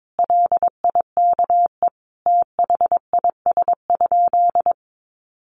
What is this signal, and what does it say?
Morse code, heard as a keyed tone, about 22 wpm, 700 Hz. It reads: LIKE THIS?
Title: Sentences from Top 1400 Words 22wpm